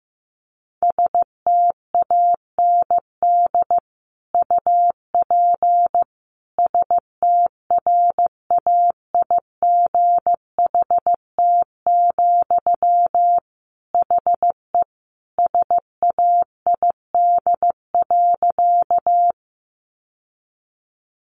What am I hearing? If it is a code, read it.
Morse code, heard as a keyed tone, about 15 wpm, 700 Hz. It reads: STAND UP STRAIGHT, HE SAID.
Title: Sentences from Top 1000 Words 15wpm